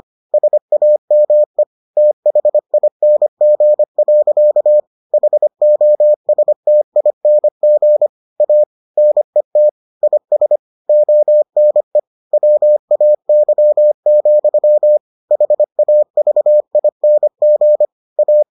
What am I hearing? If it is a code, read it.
Morse code, heard as a keyed tone, about 25 wpm, 600 Hz. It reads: SAME THING. HOSTING A NET IS ONE WAY, HAVING A